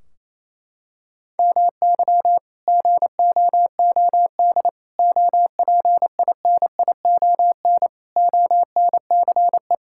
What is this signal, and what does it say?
Morse code, heard as a keyed tone, about 28 wpm, 700 Hz. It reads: MY GOOD OPINION ONCE
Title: Sentences from Top 1600 Words 28wpm